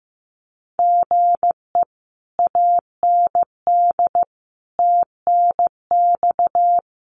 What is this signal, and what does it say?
Morse code, heard as a keyed tone, about 15 wpm, 700 Hz. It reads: GE AND TNX
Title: Common QSO Phrases Encore 15wpm